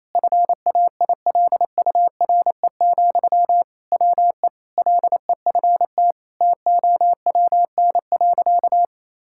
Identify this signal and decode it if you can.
Morse code, heard as a keyed tone, about 28 wpm, 700 Hz. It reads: FAILURE, WE LEFT TOWN.
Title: Sentences from Top 1000 Words - Review Part 6 - 28wpm